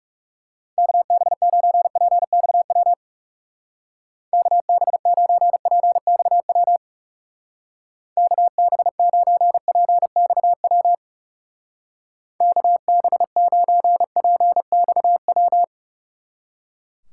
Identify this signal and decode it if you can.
Morse code, heard as a keyed tone, about 37 wpm, 700 Hz. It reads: KB9PXW KB9PXW KB9PXW KB9PXW
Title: Speed Racing - USA 2x3 Call Signs 30wpm